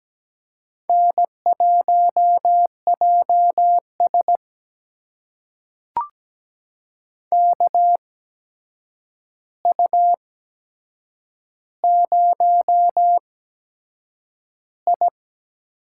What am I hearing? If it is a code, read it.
Morse code, heard as a keyed tone, about 17 wpm, spaced as for 7 wpm, 700 Hz. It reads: N1JS KU0I
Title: USA Call Signs - 4 Characters - 8x Character Spacing - 17wpm